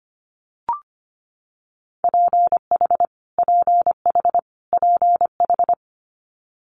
Morse code, heard as a keyed tone, about 25 wpm, 700 Hz. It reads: PH PH PH
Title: Multi-letter Phonemes Repeated 3x 25wpm